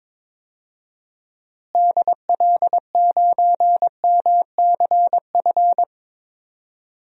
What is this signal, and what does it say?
Morse code, heard as a keyed tone, about 22 wpm, 700 Hz. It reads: DL9MCF
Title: DXCC-100 Call Signs - 6 Characters - Repeated 3x - 22wpm